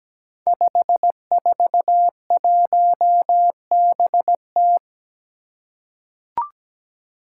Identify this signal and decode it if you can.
Morse code, heard as a keyed tone, about 17 wpm, 700 Hz. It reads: 541BT
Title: Five Challenging Letter-Number Combinations 17wpm